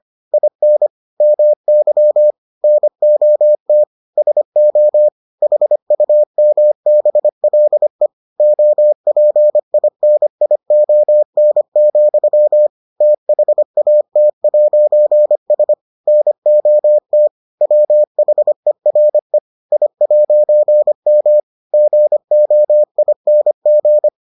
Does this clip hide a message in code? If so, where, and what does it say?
Yes: Morse code from 0.3 to 24.1 s, heard as a keyed tone, about 25 wpm, 600 Hz. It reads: IN MY NOT SO HUMBLE OPINION, THAT'S NOT WHERE I'M GOING